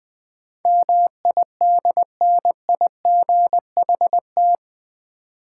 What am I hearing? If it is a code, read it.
Morse code, heard as a keyed tone, about 20 wpm, 700 Hz. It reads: MIDNIGHT
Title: 8 Letter Words - First 3 Letters Given - 20wpm